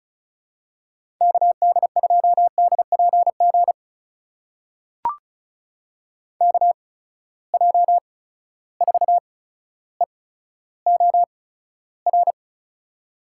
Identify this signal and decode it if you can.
Morse code, heard as a keyed tone, about 36 wpm, spaced as for 15 wpm, 700 Hz. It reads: KD2DPG KJ4EOR
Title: USA Call Signs - 6 Characters - 8x Character Spacing - 35wpm